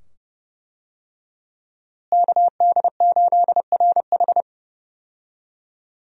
Morse code, heard as a keyed tone, about 32 wpm, 700 Hz. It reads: KD8RH
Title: USA Call Signs - 5 Characters - Repeated 2x - 30wpm